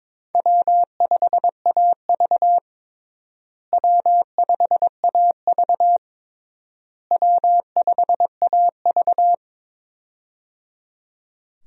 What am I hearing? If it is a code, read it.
Morse code, heard as a keyed tone, about 22 wpm, 700 Hz. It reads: W5AV W5AV W5AV
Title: USA Call Signs - 4 Characters - Repeated 3x - 22wpm